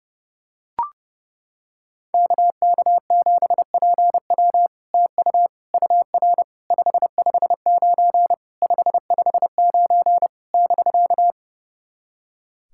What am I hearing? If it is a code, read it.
Morse code, heard as a keyed tone, about 30 wpm, 700 Hz. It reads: KK7PW TU UR 559 559 <BK>
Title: SOTA QSO Practice 30wpm